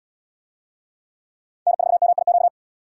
Morse code, heard as a keyed tone, about 64 wpm, 700 Hz. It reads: N4MIQ